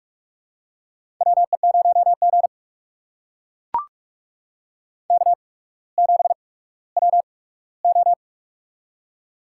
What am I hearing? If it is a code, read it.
Morse code, heard as a keyed tone, about 45 wpm, spaced as for 19 wpm, 690 Hz. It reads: WE0G K7WO
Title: USA Call Signs - 4 Characters - 8x Character Spacing - 45wpm